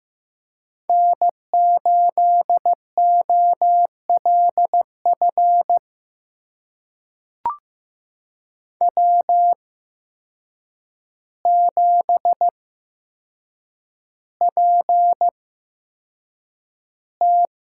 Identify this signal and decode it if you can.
Morse code, heard as a keyed tone, about 15 wpm, spaced as for 7 wpm, 700 Hz. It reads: N8OLF W7PT
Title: USA Call Signs - 5 Characters - 8x Character Spacing - 15wpm